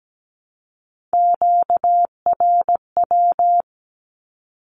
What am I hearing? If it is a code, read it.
Morse code, heard as a keyed tone, about 17 wpm, 700 Hz. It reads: QRW
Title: Q-Codes Repeated 3x 17wpm